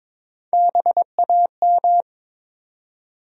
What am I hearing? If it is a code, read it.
Morse code, heard as a keyed tone, about 22 wpm, 700 Hz. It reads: BAM
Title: USA Call Sign Postfixes - 3 Letters 22wpm